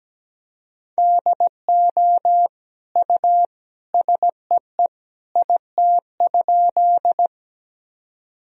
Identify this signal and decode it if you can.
Morse code, heard as a keyed tone, about 17 wpm, 700 Hz. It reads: DO U SEE IT?
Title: Sentences from Top 100 Words and Abbreviations 17wpm